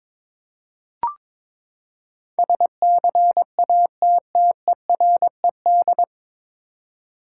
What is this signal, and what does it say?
Morse code, heard as a keyed tone, about 22 wpm, 700 Hz. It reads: SCATTERED
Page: Morse Code Ninja practice 22wpm